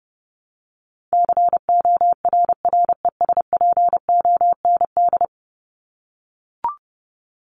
Morse code, heard as a keyed tone, about 30 wpm, 700 Hz. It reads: CORRESPOND
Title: ING Suffix - Top 100 Words with Base Words 30wpm